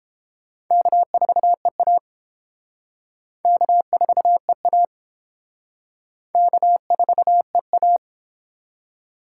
Morse code, heard as a keyed tone, about 29 wpm, 700 Hz. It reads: K4EA K4EA K4EA